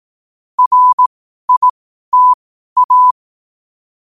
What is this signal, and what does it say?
Morse code, heard as a keyed tone, about 18 wpm, spaced as for 12 wpm, 1000 Hz. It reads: RITA
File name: qotdfiles / 12wpm